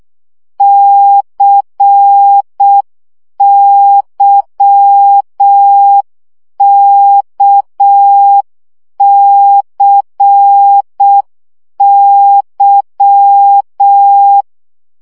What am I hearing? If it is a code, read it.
Morse code, heard as a keyed tone, about 6 wpm, 800 Hz. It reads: CYKCY
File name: CW gyakorlas k c y 5 baud